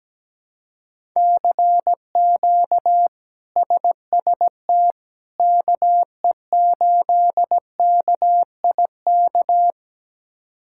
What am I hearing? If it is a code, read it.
Morse code, heard as a keyed tone, about 17 wpm, 700 Hz. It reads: CQ SST KE8KIK
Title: K1USN SST QSO Practice - 17wpm